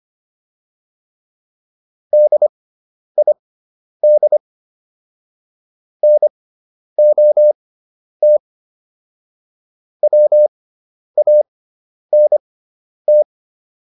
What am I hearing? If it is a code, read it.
Morse code, heard as a keyed tone, about 25 wpm, spaced as for 10 wpm, 600 Hz. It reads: DID NOT WANT